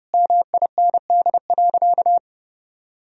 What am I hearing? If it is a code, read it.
Morse code, heard as a keyed tone, about 30 wpm, 700 Hz. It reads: MIND.
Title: Sentences from Top 1000 Words - Review Part 4 - 30wpm